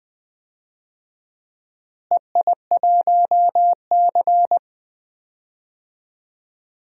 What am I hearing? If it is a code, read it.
Morse code, heard as a keyed tone, about 20 wpm, 700 Hz. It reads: EI1C
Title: DXCC-100 Call Signs - 4 Characters - Repeated 2x - 20wpm